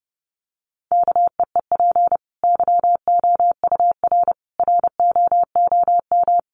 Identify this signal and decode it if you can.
Morse code, heard as a keyed tone, about 30 wpm, 700 Hz. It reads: KEEP YOUR ROOM
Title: Sentences from Top 1100 Words 30wpm